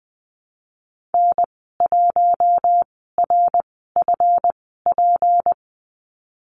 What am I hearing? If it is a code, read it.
Morse code, heard as a keyed tone, about 20 wpm, spaced as for 15 wpm, 700 Hz. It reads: N1RFP
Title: 186 Lesson 30g - Letter b - Call Signs 20wpm